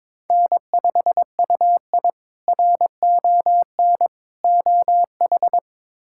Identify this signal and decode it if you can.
Morse code, heard as a keyed tone, about 22 wpm, 700 Hz. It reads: N5UI RON OH
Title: K1USN SST QSO Practice - 22wpm